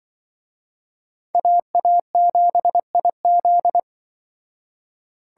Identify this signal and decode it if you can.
Morse code, heard as a keyed tone, about 24 wpm, 700 Hz. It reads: AA7IZ
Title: Speed Racing - USA 2x2 Call Signs 20wpm